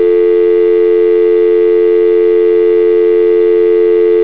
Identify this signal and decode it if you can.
Dial tone (North America), 350 + 440 Hz, with noise.
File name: Motorola / dialtone